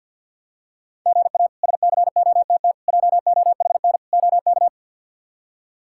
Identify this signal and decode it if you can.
Morse code, heard as a keyed tone, about 49 wpm, 700 Hz. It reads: GA SCOTT JOHN OK
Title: K1USN SST QSO Practice - 50wpm